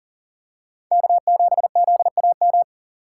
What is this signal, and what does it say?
Morse code, heard as a keyed tone, about 40 wpm, 700 Hz. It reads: K7ZAM
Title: USA Call Signs - 5 Characters - 4x Character Spacing - 40wpm